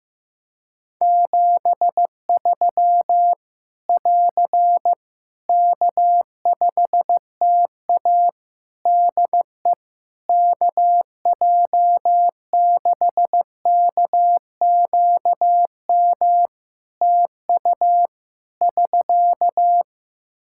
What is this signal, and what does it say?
Morse code, heard as a keyed tone, about 15 wpm, 700 Hz. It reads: 73 <AR> K5TA DE KJ6KQM TU <SK>
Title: Practice QSO - Exchange 4 - Ending - 15wpm